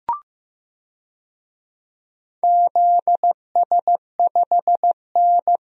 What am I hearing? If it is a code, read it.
Morse code, heard as a keyed tone, about 15 wpm, 700 Hz. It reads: ZS5N
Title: DXCC-100 Call Signs - 4 Characters - Repeated 3x - 15wpm